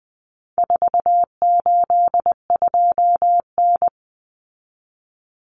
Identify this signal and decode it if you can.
Morse code, heard as a keyed tone, about 20 wpm, 700 Hz. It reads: 482N
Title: Common Cut Numbers - 4-Digit 20wpm